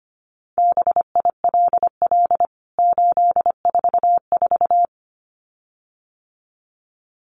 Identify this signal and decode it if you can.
Morse code, heard as a keyed tone, about 25 wpm, 700 Hz. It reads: BILL 844
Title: CWT Exchange 25wpm